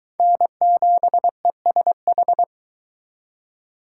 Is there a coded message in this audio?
Yes: Morse code from 0.2 to 2.4 s, heard as a keyed tone, about 24 wpm, 700 Hz. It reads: N7ESH